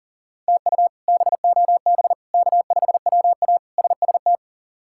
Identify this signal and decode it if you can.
Morse code, heard as a keyed tone, about 40 wpm, 690 Hz. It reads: TU BOB K5WA SST
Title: K1USN SST QSO Practice - 40wpm